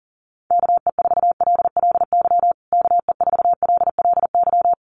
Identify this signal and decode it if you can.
Morse code, heard as a keyed tone, about 40 wpm, 700 Hz. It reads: KE4LLY KE4LLY
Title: USA Call Signs Repeated 3x 40wpm